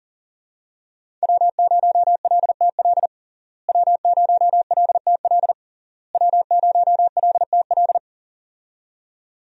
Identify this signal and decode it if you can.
Morse code, heard as a keyed tone, about 40 wpm, 700 Hz. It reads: W0LTL W0LTL W0LTL
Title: USA Call Signs - 5 Characters - Repeated 3x - 40wpm